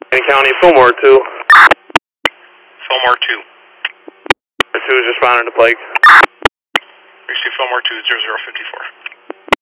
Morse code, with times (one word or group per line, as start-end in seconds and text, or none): none